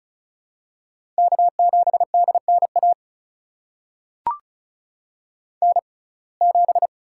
Morse code, heard as a keyed tone, about 36 wpm, spaced as for 25 wpm, 700 Hz. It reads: K7DNA N7
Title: USA Call Signs - 5 Characters - 6x Character Spacing - 35wpm